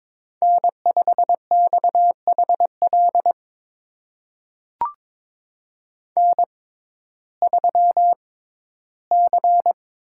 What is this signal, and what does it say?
Morse code, heard as a keyed tone, about 22 wpm, spaced as for 13 wpm, 700 Hz. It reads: N5XHL N3C
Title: USA Call Signs - 5 Characters - 6x Character Spacing - 22wpm